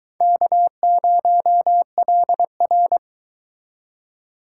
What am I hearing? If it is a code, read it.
Morse code, heard as a keyed tone, about 23 wpm, 700 Hz. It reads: K0LR